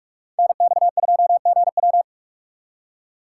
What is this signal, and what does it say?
Morse code, heard as a keyed tone, about 58 wpm, 700 Hz. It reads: NX2GW